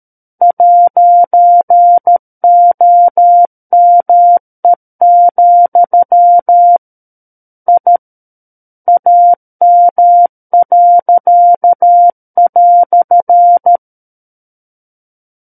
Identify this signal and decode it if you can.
Morse code, heard as a keyed tone, about 13 wpm, 700 Hz. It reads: 'OME, I AM."
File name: Book One / 04-Chapter 4 - 13 WPM